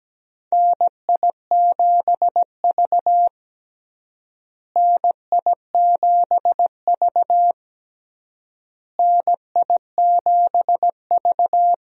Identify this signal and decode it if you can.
Morse code, heard as a keyed tone, about 17 wpm, 700 Hz. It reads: NI7V NI7V NI7V
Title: USA Call Signs - 4 Characters - Repeated 3x - 17wpm